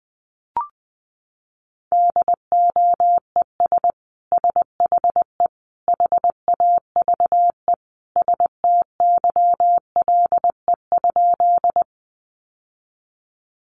Morse code, heard as a keyed tone, about 20 wpm, 700 Hz. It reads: DOES SHE HAVE STYLE?
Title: Sentences from Top 800 Words 20wpm